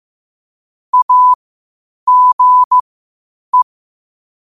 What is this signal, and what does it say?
Morse code, heard as a keyed tone, about 15 wpm, spaced as for 8 wpm, 1000 Hz. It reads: AGE